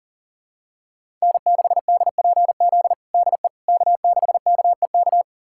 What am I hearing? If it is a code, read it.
Morse code, heard as a keyed tone, about 37 wpm, 700 Hz. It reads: N6DPZ DE K6KEK
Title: USA Callsign DE Callsign - 5 Characters - 40wpm